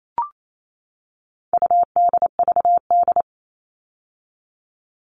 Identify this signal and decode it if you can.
Morse code, heard as a keyed tone, about 28 wpm, 700 Hz. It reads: UDVD
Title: All Cut Numbers - 4-Digit 28wpm